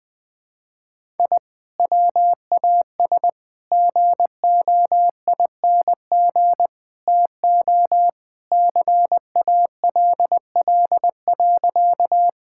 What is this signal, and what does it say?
Morse code, heard as a keyed tone, about 20 wpm, 700 Hz. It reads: I WAS GOING TO CALL.
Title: Sentences with 4-Word Phrases - 20wpm